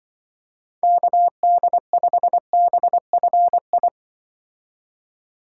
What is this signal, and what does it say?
Morse code, heard as a keyed tone, about 23 wpm, 700 Hz. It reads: KD5BFI
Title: Speed Racing - USA 2x3 Call Signs 20wpm